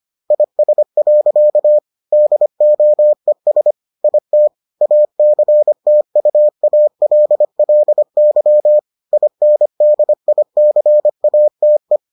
Morse code, heard as a keyed tone, about 25 wpm, 600 Hz. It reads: IS. DOES IT ACTUALLY INDICATE